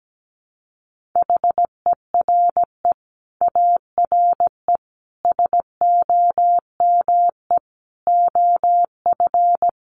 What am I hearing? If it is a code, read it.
Morse code, heard as a keyed tone, about 17 wpm, 700 Hz. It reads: HERE ARE SOME OF
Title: Sets of 5 Words Encore 17wpm